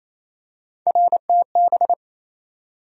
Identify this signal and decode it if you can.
Morse code, heard as a keyed tone, about 28 wpm, 700 Hz. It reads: RTB
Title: USA Call Sign Postfixes - 3 Letters 28wpm